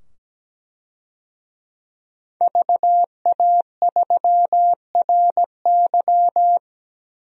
Morse code, heard as a keyed tone, about 17 wpm, 700 Hz. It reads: VA3RY
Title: DXCC-100 Call Signs - 5 Characters - 17wpm